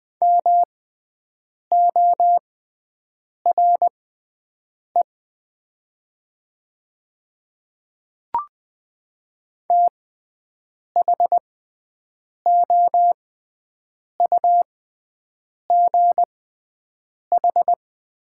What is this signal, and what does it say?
Morse code, heard as a keyed tone, about 20 wpm, spaced as for 7 wpm, 700 Hz. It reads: MORE THOUGH
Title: 6x Character Spacing - Top 500 Words - 20wpm